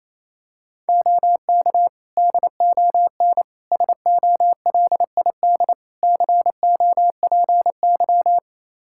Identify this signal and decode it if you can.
Morse code, heard as a keyed tone, about 28 wpm, 700 Hz. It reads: OK DON SOLID COPY
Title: Practice QSO - Exchange 3 - More Info - 28wpm